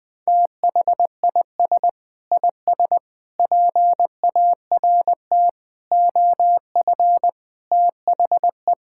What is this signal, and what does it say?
Morse code, heard as a keyed tone, about 20 wpm, 700 Hz. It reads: THIS IS PART OF THE
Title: Sentences with 2-Word Phrases - 20wpm